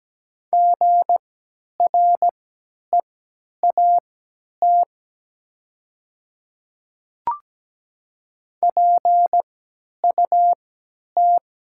Morse code, heard as a keyed tone, about 17 wpm, spaced as for 10 wpm, 700 Hz. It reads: GREAT PUT
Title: 3x Character Spacing - Top 200 Words - 17wpm